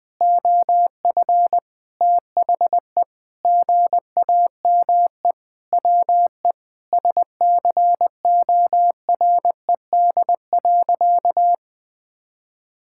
Morse code, heard as a keyed tone, about 21 wpm, 700 Hz. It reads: OF THE GAME WE SCORED.
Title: Sentences with 6-Word Phrases - 20wpm